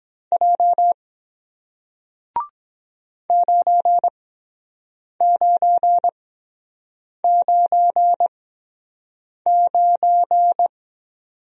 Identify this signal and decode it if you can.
Morse code, heard as a keyed tone, about 23 wpm, spaced as for 7 wpm, 700 Hz. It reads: J 9999